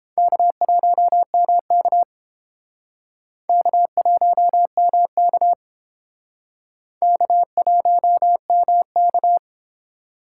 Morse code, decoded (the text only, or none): K1MK K1MK K1MK